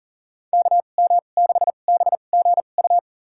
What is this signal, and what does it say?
Morse code, heard as a keyed tone, about 37 wpm, spaced as for 28 wpm, 700 Hz. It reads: KM6BGU
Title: USA Call Signs - 6 Characters - 2x Character Spacing - 40wpm